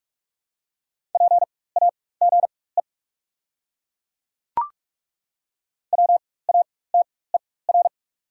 Morse code, heard as a keyed tone, about 42 wpm, spaced as for 20 wpm, 690 Hz. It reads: PAGE WATER
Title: 4x Character Spacing - Top 500 Words - 45wpm